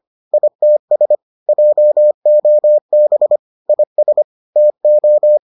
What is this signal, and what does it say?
Morse code, heard as a keyed tone, about 25 wpm, 600 Hz. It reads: ITS JOB IS TO